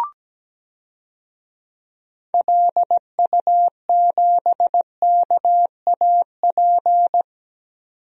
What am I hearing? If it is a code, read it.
Morse code, heard as a keyed tone, about 17 wpm, 700 Hz. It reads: LU7KAP
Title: DXCC-100 Call Signs - 6 Characters - 17wpm